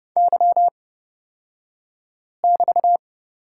Morse code, heard as a keyed tone, about 31 wpm, spaced as for 5 wpm, 700 Hz. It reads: Y<BT>
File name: Lesson 36d - BT - Two Characters 30wpm